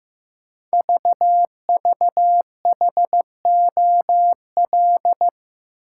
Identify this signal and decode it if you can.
Morse code, heard as a keyed tone, about 15 wpm, 700 Hz. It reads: VVHOL
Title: Code Groups - 5 Letters - 15wpm